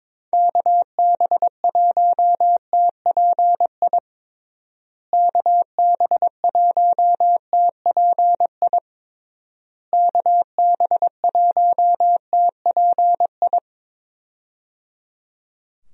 Morse code, heard as a keyed tone, about 23 wpm, 700 Hz. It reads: KB1TPI KB1TPI KB1TPI
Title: USA Call Signs - 6 Characters - Repeated 3x - 22wpm